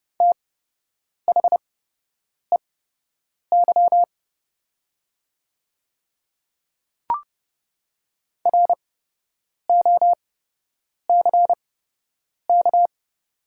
Morse code, heard as a keyed tone, about 30 wpm, spaced as for 8 wpm, 700 Hz. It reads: THEY ROCK
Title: 8x Character Spacing - Top 500 Words - 30wpm